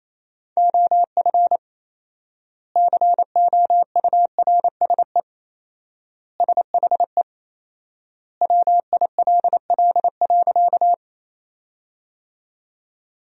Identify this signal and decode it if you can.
Morse code, heard as a keyed tone, about 28 wpm, 700 Hz. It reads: OF COURSE SHE WILL.
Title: 4x Word-Spacing - Sentences from Top 500 Words - 28wpm